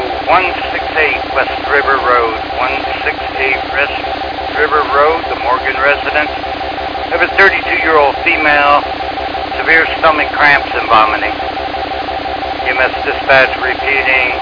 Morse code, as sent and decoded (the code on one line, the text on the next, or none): none